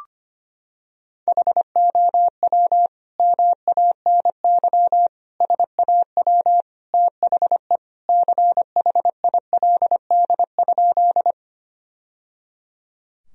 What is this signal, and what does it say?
Morse code, heard as a keyed tone, about 25 wpm, 700 Hz. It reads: HOW MANY SAW THE CHILD?
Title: Sentences from Top 1000 Words - Review Part 1 - 25wpm